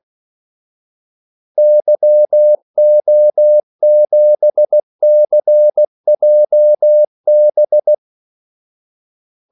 Morse code, heard as a keyed tone, about 16 wpm, 600 Hz. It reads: YO7CJB
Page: Kaldesignaler 16 wpm